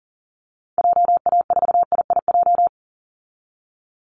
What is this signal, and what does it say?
Morse code, heard as a keyed tone, about 40 wpm, 700 Hz. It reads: JA4IIJ
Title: Call Signs Repeated 3x - 40wpm